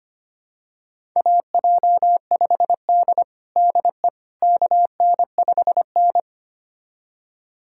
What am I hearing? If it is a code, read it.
Morse code, heard as a keyed tone, about 25 wpm, 700 Hz. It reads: AJ5D DE KN5N
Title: USA Callsign DE Callsign - 4 Characters - 25wpm